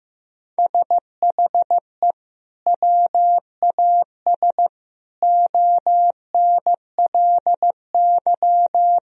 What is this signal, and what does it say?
Morse code, heard as a keyed tone, about 15 wpm, 700 Hz. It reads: SHE WAS ONLY
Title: Sentences from Top 1000 Words - Review Part 4 - 15wpm